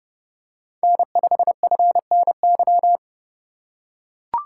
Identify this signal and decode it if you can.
Morse code, heard as a keyed tone, about 30 wpm, 700 Hz. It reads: N5FNY
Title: USA Call Signs - 5 Characters - 2x Character Spacing - 30wpm